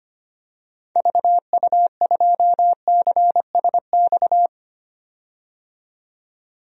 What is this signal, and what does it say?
Morse code, heard as a keyed tone, about 25 wpm, 700 Hz. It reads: VU2CSX
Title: DXCC-100 Call Signs - 6 Characters - 25wpm